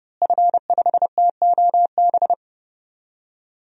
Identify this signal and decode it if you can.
Morse code, heard as a keyed tone, about 29 wpm, 700 Hz. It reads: F5TOB
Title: DXCC-100 Call Signs - 5 Characters - Repeated 3x - 30wpm